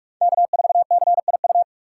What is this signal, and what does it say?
Morse code, heard as a keyed tone, about 44 wpm, 700 Hz. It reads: K4CIU